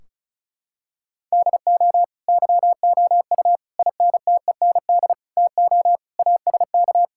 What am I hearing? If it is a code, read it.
Morse code, heard as a keyed tone, about 36 wpm, 700 Hz. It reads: DO YOU INTEND TO ASK
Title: Sentences from Top 1600 Words 35wpm